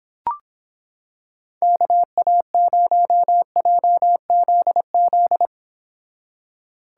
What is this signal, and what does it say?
Morse code, heard as a keyed tone, about 26 wpm, 700 Hz. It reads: KA0JZZ